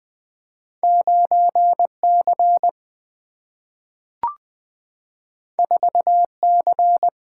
Morse code, heard as a keyed tone, about 20 wpm, 700 Hz. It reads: 9C 4C
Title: Single-Letter Single-Number 20wpm